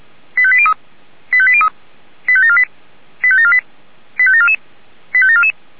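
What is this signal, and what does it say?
Five-tone selective call (ZVEI), heard as quick runs of five tones, tones of about 70 ms. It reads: 85803 85803 85748 85748 85744 85744
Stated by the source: ZVEI five-tone selective call 85719 received on an SDR.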